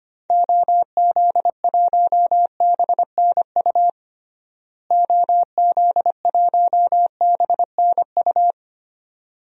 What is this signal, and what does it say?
Morse code, heard as a keyed tone, about 25 wpm, 700 Hz. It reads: OZ1BNU OZ1BNU
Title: DXCC-100 Call Signs - 6 Characters - Repeated 2x - 25wpm